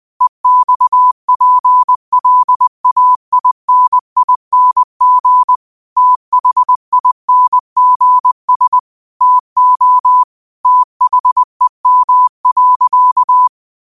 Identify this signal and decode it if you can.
Morse code, heard as a keyed tone, about 20 wpm, 1000 Hz. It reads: EXPLAINING THINGS TO THEM.
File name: qotdfiles / 20wpm